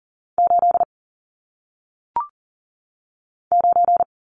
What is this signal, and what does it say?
Morse code, heard as a keyed tone, about 41 wpm, 700 Hz. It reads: 8 9
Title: Numbers - 1-Digit 40wpm